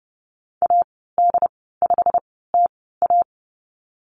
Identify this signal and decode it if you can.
Morse code, heard as a keyed tone, about 30 wpm, spaced as for 17 wpm, 700 Hz. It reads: AD5TA
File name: Lesson 15g - Number 5 - Call Signs 30wpm